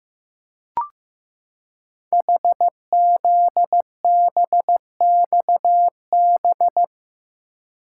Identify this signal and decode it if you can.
Morse code, heard as a keyed tone, about 15 wpm, 690 Hz. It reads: HZBXB
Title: Code Groups - 5 Letters - 15wpm